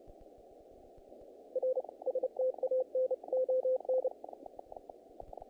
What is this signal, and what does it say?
Morse code, heard as a keyed tone, about 31 wpm, 520 Hz, with radio noise. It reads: R STAN ON
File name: contests / naqp-cw-k5tr-op-k5pi- -1900